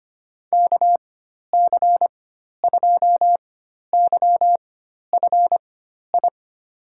Morse code, heard as a keyed tone, about 25 wpm, spaced as for 12 wpm, 700 Hz. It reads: KC2YFI
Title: USA Call Signs - 6 Characters - 4x Character Spacing - 25wpm